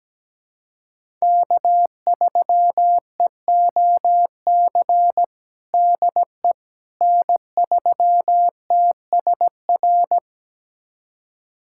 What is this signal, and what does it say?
Morse code, heard as a keyed tone, about 17 wpm, 700 Hz. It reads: K3EOC DE N3TSR
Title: USA Callsign DE Callsign - 5 Characters - 17wpm